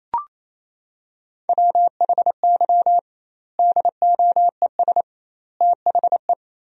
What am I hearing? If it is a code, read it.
Morse code, heard as a keyed tone, about 28 wpm, 700 Hz. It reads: WHY DOES THE
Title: 2x Word-Spacing - Sentences from Top 500 Words - 28wpm